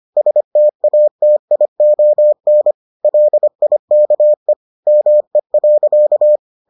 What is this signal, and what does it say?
Morse code, heard as a keyed tone, about 25 wpm, 600 Hz. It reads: STATION LIKE ME.